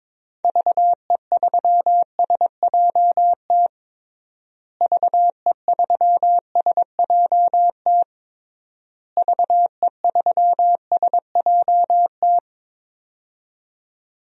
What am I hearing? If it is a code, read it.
Morse code, heard as a keyed tone, about 22 wpm, 700 Hz. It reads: VE3SJT VE3SJT VE3SJT
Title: DXCC-100 Call Signs - 6 Characters - Repeated 3x - 22wpm